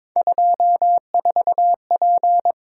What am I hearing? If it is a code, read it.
Morse code, heard as a keyed tone, about 22 wpm, 700 Hz. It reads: 24P
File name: practice / 22wpm / Three Letter-Number Combinations - 22wpm